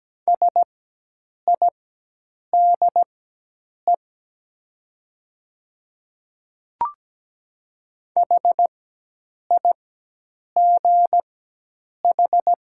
Morse code, heard as a keyed tone, about 17 wpm, spaced as for 8 wpm, 700 Hz. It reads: SIDE HIGH
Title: 4x Character Spacing - Top 200 Words - 17wpm